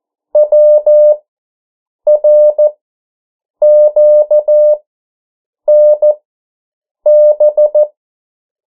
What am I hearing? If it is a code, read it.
Morse code, heard as a keyed tone, about 14 wpm, spaced as for 7 wpm, 600 Hz. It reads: WRQNB
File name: morse mp3 / 09 urok 8